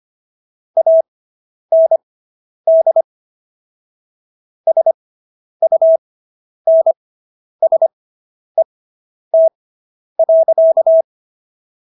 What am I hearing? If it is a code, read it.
Morse code, heard as a keyed tone, about 25 wpm, spaced as for 10 wpm, 650 Hz. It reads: AND SUNSET.